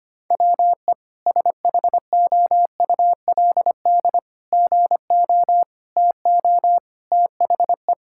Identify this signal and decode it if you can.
Morse code, heard as a keyed tone, about 25 wpm, 700 Hz. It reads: WE SHOULD GO TO THE
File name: practice / 25wpm / Sentences with 3-Word Phrases - 25wpm